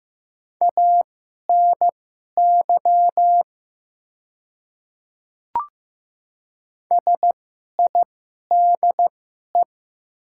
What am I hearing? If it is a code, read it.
Morse code, heard as a keyed tone, about 15 wpm, spaced as for 11 wpm, 700 Hz. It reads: ANY SIDE
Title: 2x Character Spacing - Top 200 Words - 15wpm